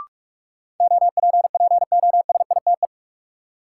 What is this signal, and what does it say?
Morse code, heard as a keyed tone, about 46 wpm, 700 Hz. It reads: OPPOSITE
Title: Sentences from Top 1300 Words - New Word Drill Alt - 45wpm